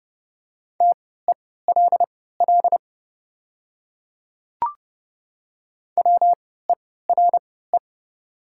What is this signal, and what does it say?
Morse code, heard as a keyed tone, about 29 wpm, spaced as for 17 wpm, 700 Hz. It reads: TELL WERE
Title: 3x Character Spacing - Top 300 Words - 30wpm